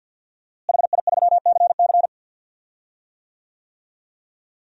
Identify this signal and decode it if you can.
Morse code, heard as a keyed tone, about 48 wpm, 700 Hz. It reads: HI3CC
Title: DXCC-100 Call Signs - 5 Characters - Repeated 3x - 50wpm